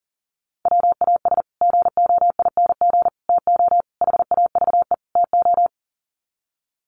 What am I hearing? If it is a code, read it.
Morse code, heard as a keyed tone, about 40 wpm, 700 Hz. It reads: WAS GOING TO HAVE TO
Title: Sets of 5 Words Encore 40wpm